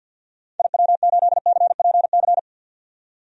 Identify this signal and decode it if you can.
Morse code, heard as a keyed tone, about 45 wpm, 700 Hz. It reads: IW8CPC
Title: DXCC-100 Call Signs - 6 Characters - 50wpm